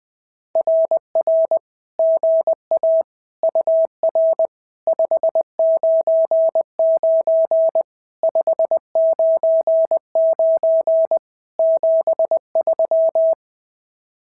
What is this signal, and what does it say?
Morse code, heard as a keyed tone, about 20 wpm, 650 Hz. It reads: RR GA UR 599 599 73
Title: SOTA QSO Practice 20wpm